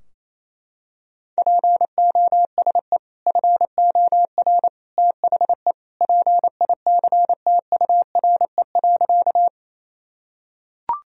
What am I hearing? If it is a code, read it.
Morse code, heard as a keyed tone, about 28 wpm, 700 Hz. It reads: POSE FOR THE PICTURE.
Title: Sentences from Top 1000 Words - Review Part 2 - 28wpm